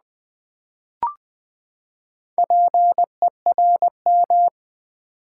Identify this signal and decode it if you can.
Morse code, heard as a keyed tone, about 20 wpm, 700 Hz. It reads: PERM
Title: Build A Word - 4 Letters At A Time - 10-Letter Words - 20wpm